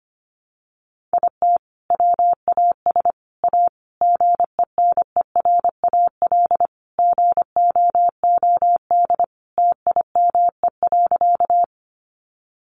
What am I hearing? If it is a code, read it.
Morse code, heard as a keyed tone, about 25 wpm, 700 Hz. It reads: IT WAS A GENERAL GOOD TIME.
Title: Sentences from Top 600 Words 25wpm